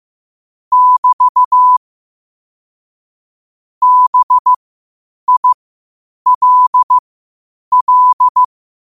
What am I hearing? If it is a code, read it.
Morse code, heard as a keyed tone, about 15 wpm, spaced as for 8 wpm, 1000 Hz. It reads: <BT> BILL